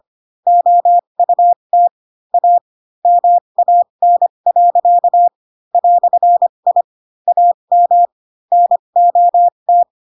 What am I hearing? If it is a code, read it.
Morse code, heard as a keyed tone, about 25 wpm, spaced as for 22 wpm, 700 Hz. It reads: OUT A MAN. "I AM NOT
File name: Pinocchio / Pinocchio ch3 18 25 ik1hge